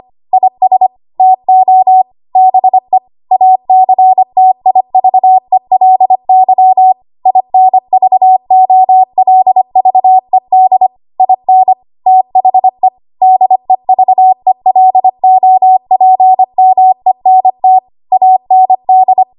Morse code, heard as a keyed tone, about 25 wpm, 750 Hz. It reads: IS TO BE ACTIVELY INVOLVED IN THE DEVELOPMENT AND
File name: Archive / 25 WPM / 25WPM